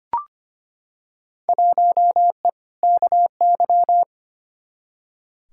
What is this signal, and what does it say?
Morse code, heard as a keyed tone, about 25 wpm, 700 Hz. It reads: 1E KY
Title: Field Day Exchanges - 25wpm